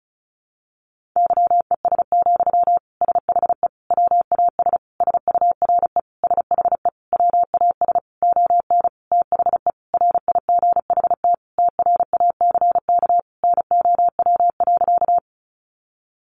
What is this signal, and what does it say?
Morse code, heard as a keyed tone, about 34 wpm, 700 Hz. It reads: YES, SHE WAS SURE SHE WAS ON THE RIGHT TRACK NOW.
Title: Sentences from Top 1100 Words 35wpm